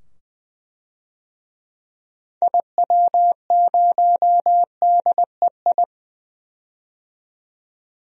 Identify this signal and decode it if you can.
Morse code, heard as a keyed tone, about 20 wpm, 700 Hz. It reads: IW0DEI
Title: DXCC-100 Call Signs - 6 Characters - 20wpm